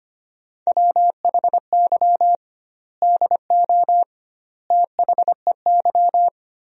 Morse code, heard as a keyed tone, about 25 wpm, 700 Hz. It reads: WHY DO THEY
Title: 2x Word-Spacing - Sentences from Top 400 Words - 25wpm